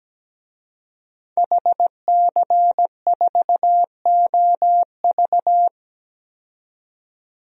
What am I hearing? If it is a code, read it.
Morse code, heard as a keyed tone, about 17 wpm, 700 Hz. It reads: HC4OV
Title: DXCC-100 Call Signs - 5 Characters - 17wpm